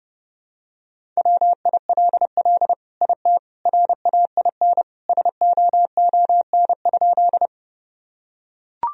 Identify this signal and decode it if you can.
Morse code, heard as a keyed tone, about 30 wpm, 700 Hz. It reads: WILL IT RAIN SOON?
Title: Sentences from Top 1000 Words - Review Part 1 - 30wpm